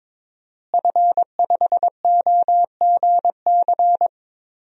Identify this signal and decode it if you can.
Morse code, heard as a keyed tone, about 22 wpm, 700 Hz. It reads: F5OGC
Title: DXCC-100 Call Signs - 5 Characters - Repeated 2x - 22wpm